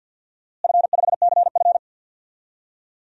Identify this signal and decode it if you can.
Morse code, heard as a keyed tone, about 52 wpm, 700 Hz. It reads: F5CF
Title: DXCC-100 Call Signs - 4 Characters - Repeated 3x - 50wpm